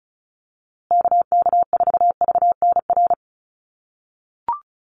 Morse code, heard as a keyed tone, about 34 wpm, 700 Hz. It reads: KK4VNR
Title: USA Call Signs Repeated 2x 35wpm